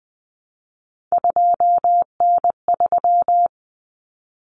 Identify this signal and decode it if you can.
Morse code, heard as a keyed tone, about 20 wpm, 700 Hz. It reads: 2N3
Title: Common Cut Numbers - 3-Digit 20wpm